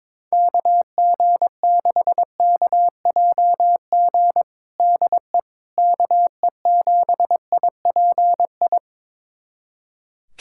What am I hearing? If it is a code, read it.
Morse code, heard as a keyed tone, about 22 wpm, 700 Hz. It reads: KG6KJG DE KE7IPI
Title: USA Callsign DE Callsign - 6 Characters - 22wpm